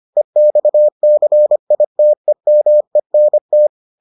Morse code, heard as a keyed tone, about 25 wpm, 600 Hz. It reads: EXCITEMENT